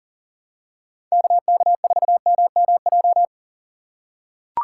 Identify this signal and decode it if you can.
Morse code, heard as a keyed tone, about 40 wpm, 700 Hz. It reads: KK4MMJ
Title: USA Call Signs - 6 Characters - 6x Character Spacing - 40wpm